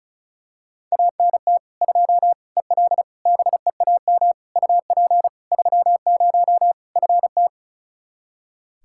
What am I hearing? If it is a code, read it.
Morse code, heard as a keyed tone, about 34 wpm, 700 Hz. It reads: ANT 2 EL BEAM UP 30 FT
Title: Practice QSO - Exchange 2 - Info - 35wpm